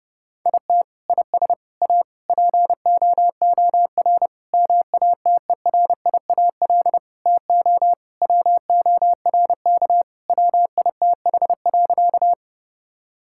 Morse code, heard as a keyed tone, about 32 wpm, 700 Hz. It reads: IT IS A POOR MATERIAL TO WORK WITH.
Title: Sentences from Top 1000 Words - Review Part 5 - 30wpm